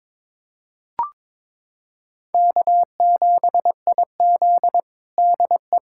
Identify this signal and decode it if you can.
Morse code, heard as a keyed tone, about 22 wpm, 700 Hz. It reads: K7IZ DE
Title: USA Callsign DE Callsign - 4 Characters - 22wpm